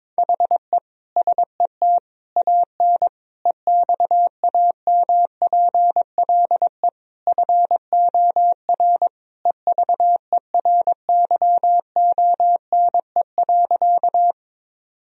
Morse code, heard as a keyed tone, about 22 wpm, 700 Hz. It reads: HE SET AN EXAMPLE FOR EVERYONE.
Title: Sentences from Top 1000 Words - Review Part 1 - 22wpm